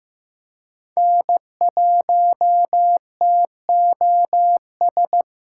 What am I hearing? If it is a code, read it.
Morse code, heard as a keyed tone, about 15 wpm, 700 Hz. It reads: N1TOS
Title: USA Callsign DE Callsign - 5 Characters - 15wpm